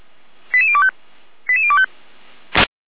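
Five-tone selective call (ZVEI), heard as quick runs of five tones, tones of about 70 ms. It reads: 88025 88025
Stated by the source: ZVEI five-tone selective call received on an SDR.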